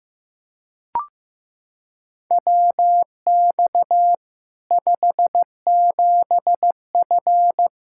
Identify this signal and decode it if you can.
Morse code, heard as a keyed tone, about 15 wpm, 700 Hz. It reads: WX 57F
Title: K1USN SST QSO Practice 15wpm